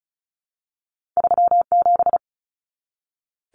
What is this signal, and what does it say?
Morse code, heard as a keyed tone, about 36 wpm, 700 Hz. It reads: 37
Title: Numbers - 2-Digit 35wpm